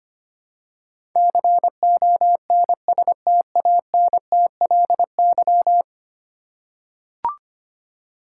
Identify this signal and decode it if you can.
Morse code, heard as a keyed tone, about 25 wpm, 700 Hz. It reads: CONSTANTLY
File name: 10 Letter Words - First 3 Letters Given - 25wpm